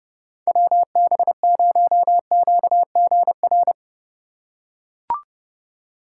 Morse code, heard as a keyed tone, about 30 wpm, 700 Hz. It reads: WB0QGR
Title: USA Call Signs - 6 Characters - 8x Character Spacing - 30wpm